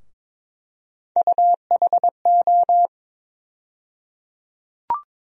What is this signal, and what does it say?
Morse code, heard as a keyed tone, about 22 wpm, 700 Hz. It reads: UHO
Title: USA Call Sign Postfixes - 3 Letters 22wpm